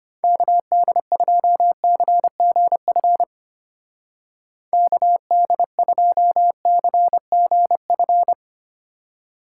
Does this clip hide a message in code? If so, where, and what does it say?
Yes: Morse code from 0.2 to 8.3 s, heard as a keyed tone, about 32 wpm, 700 Hz. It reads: KD2CGF KD2CGF